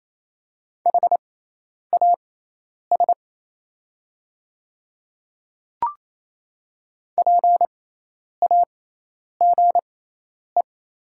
Morse code, heard as a keyed tone, about 28 wpm, spaced as for 10 wpm, 700 Hz. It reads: HAS PAGE
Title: 6x Character Spacing - Top 300 Words - 28wpm